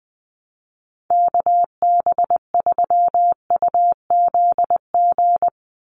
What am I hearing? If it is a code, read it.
Morse code, heard as a keyed tone, about 20 wpm, 700 Hz. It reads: KB3UZG
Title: USA Call Signs 20wpm